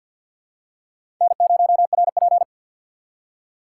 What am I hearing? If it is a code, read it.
Morse code, heard as a keyed tone, about 52 wpm, 700 Hz. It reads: N0RP
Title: USA Call Signs - 6x Character Spacing - 50wpm